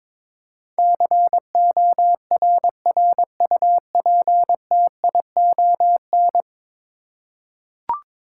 Morse code, heard as a keyed tone, about 22 wpm, 700 Hz. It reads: CORRUPTION
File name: practice / 22wpm / 10 Letter Words - First 3 Letters Given - 22wpm